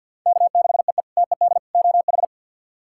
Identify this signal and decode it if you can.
Morse code, heard as a keyed tone, about 49 wpm, 700 Hz. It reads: K6EE TED OH
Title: K1USN SST QSO Practice - 50wpm